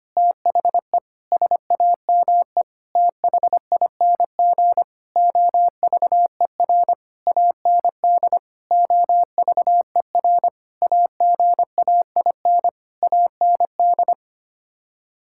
Morse code, heard as a keyed tone, about 25 wpm, 700 Hz. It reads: THE SAME THING OVER AND OVER AGAIN AND
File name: Sets of 9 Words - 25wpm